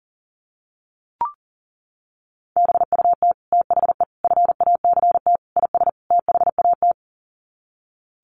Morse code, heard as a keyed tone, about 37 wpm, 700 Hz. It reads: BUT THE FACT IS THAT
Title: Sets of 5 Words Encore2 40wpm